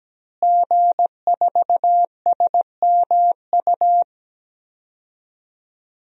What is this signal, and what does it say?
Morse code, heard as a keyed tone, about 17 wpm, 700 Hz. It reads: G4SMU
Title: DXCC-100 Call Signs - 5 Characters - Repeated 3x - 17wpm